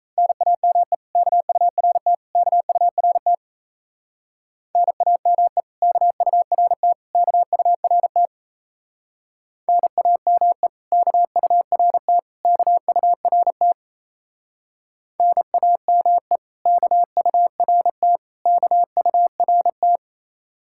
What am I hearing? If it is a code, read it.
Morse code, heard as a keyed tone, about 34 wpm, 700 Hz. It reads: NAME KURT KURT NAME KURT KURT NAME KURT KURT NAME KURT KURT
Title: Speed-Racing Common QSO Phrases - 28wpm